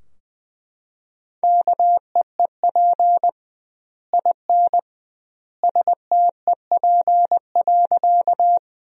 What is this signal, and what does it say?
Morse code, heard as a keyed tone, about 21 wpm, 700 Hz. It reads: KEEP IN STEP.
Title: 2x Word-Spacing - Sentences from Top 400 Words - 20wpm